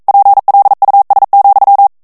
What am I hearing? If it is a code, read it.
Morse code, heard as a keyed tone, about 42 wpm, 800 Hz. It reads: PLUS,
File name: FRANCAIS TEXTE CLAIR 1 VITESSE 40 MOTS MINUTES